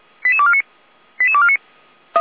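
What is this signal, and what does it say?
Five-tone selective call (ZVEI), heard as quick runs of five tones, tones of about 70 ms. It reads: 88249 88249
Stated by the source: ZVEI five-tone selective call received on SDR, from feuerwehr-orlamuende.org fire brigade server.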